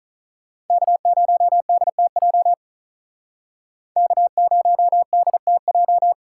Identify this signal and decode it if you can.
Morse code, heard as a keyed tone, about 46 wpm, 700 Hz. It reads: K0DTJ K0DTJ